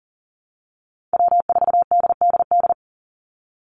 Morse code, heard as a keyed tone, about 38 wpm, 700 Hz. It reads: W4DDD
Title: Call Signs - 40wpm